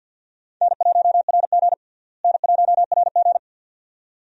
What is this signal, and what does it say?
Morse code, heard as a keyed tone, about 49 wpm, 700 Hz. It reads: N1RG N1RG
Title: USA Call Signs - 4 Characters - Repeated 2x - 50wpm